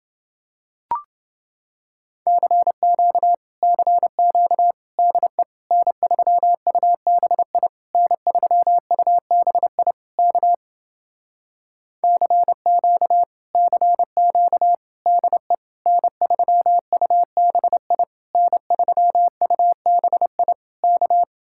Morse code, heard as a keyed tone, about 30 wpm, 700 Hz. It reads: CQ CQ DE N3UBI N3UBI K CQ CQ DE N3UBI N3UBI K